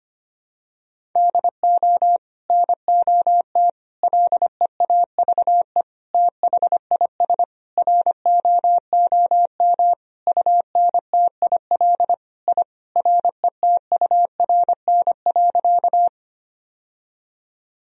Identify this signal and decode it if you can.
Morse code, heard as a keyed tone, about 25 wpm, 700 Hz. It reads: DO NOT LEAVE THIS ROOM UNTIL I RETURN.
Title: Sentences from Top 1000 Words - Review Part 1 - 25wpm